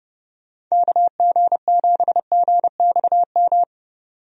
Morse code, heard as a keyed tone, about 31 wpm, 700 Hz. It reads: KG7GXM
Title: USA Call Signs - 6 Characters - 30wpm